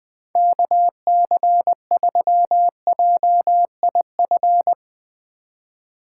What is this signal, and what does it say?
Morse code, heard as a keyed tone, about 20 wpm, 700 Hz. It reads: KC3JIF
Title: USA Call Signs - 6 Characters - 3x Character Spacing - 20wpm